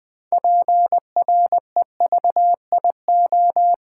Morse code, heard as a keyed tone, about 20 wpm, 700 Hz. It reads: PREVIO
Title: Build A Word - 3 Letters At A Time - 8-Letter Words - 20wpm